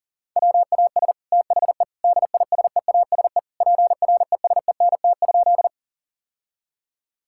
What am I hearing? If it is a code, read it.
Morse code, heard as a keyed tone, about 40 wpm, 700 Hz. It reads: WAS THE DISEASE PRESENT?
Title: Sentences from Top 1000 Words - Review Part 6 - 40wpm